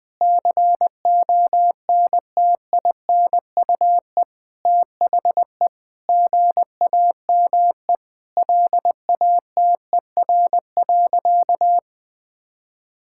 Morse code, heard as a keyed tone, about 20 wpm, 700 Hz. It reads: CONTINUE THE GAME LATER.
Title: Sentences from Top 1000 Words - Review Part 6 - 20wpm